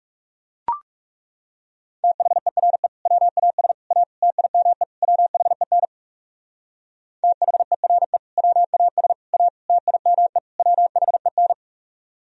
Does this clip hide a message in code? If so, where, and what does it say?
Yes: Morse code from 2.0 to 11.5 s, heard as a keyed tone, about 38 wpm, 700 Hz. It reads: THERE WAS A TIME WHEN THERE WAS A TIME WHEN